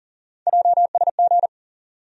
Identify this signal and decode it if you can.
Morse code, heard as a keyed tone, about 40 wpm, 700 Hz. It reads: JSG
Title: USA Call Sign Postfixes - 3 Letters 40wpm